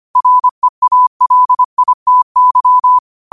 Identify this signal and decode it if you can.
Morse code, heard as a keyed tone, about 25 wpm, 1000 Hz. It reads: REALITY
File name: qotdfiles / 25wpm